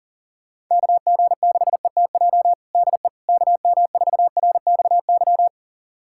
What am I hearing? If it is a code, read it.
Morse code, heard as a keyed tone, about 40 wpm, 700 Hz. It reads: KG6ETJ DE KM4RXY
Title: USA Callsign DE Callsign 40wpm